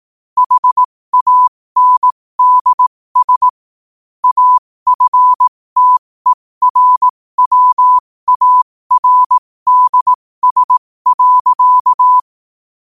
Morse code, heard as a keyed tone, about 18 wpm, spaced as for 15 wpm, 1000 Hz. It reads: HANDS AFTERWARDS.
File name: qotdfiles / 15wpm